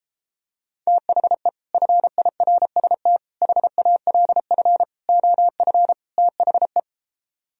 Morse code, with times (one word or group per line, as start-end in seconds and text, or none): 0.9-1.5 THE
1.7-3.2 FIRST
3.4-4.8 HALF
5.1-5.9 OF
6.2-6.8 THE